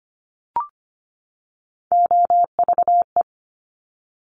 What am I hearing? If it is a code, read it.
Morse code, heard as a keyed tone, about 25 wpm, 700 Hz. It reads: OVE
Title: Top Three Letter Combinations in Words 25wpm